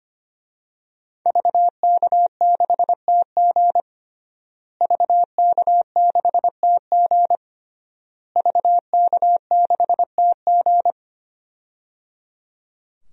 Morse code, heard as a keyed tone, about 25 wpm, 700 Hz. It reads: VK6TG VK6TG VK6TG
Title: DXCC-100 Call Signs - 5 Characters - Repeated 3x - 25wpm